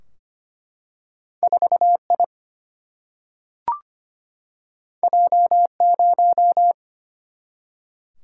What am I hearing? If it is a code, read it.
Morse code, heard as a keyed tone, about 25 wpm, 700 Hz. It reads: 4 I J 0
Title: Single-Letter Single-Number 25wpm